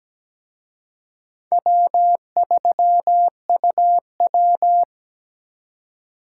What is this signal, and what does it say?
Morse code, heard as a keyed tone, about 20 wpm, 700 Hz. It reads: W3UW
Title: Speed Racing - USA 1x2 Call Signs 17wpm